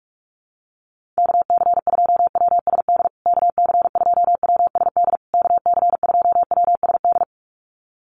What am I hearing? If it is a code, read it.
Morse code, heard as a keyed tone, about 44 wpm, 700 Hz. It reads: KC2WSD KC2WSD KC2WSD
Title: USA Call Signs Repeated 3x 45wpm